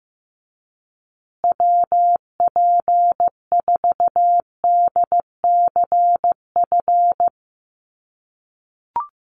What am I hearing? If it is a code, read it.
Morse code, heard as a keyed tone, about 15 wpm, 700 Hz. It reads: WP4DCF
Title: USA Call Signs Repeated 2x 15wpm